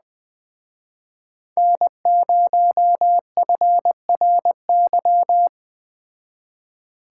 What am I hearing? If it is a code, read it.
Morse code, heard as a keyed tone, about 19 wpm, 700 Hz. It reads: N0FRY
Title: Speed Racing - USA 1x3 Call Signs 17wpm